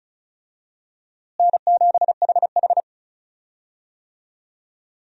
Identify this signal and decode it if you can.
Morse code, heard as a keyed tone, about 36 wpm, 700 Hz. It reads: N7HH
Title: USA Call Signs - 4 Characters - Repeated 2x - 35wpm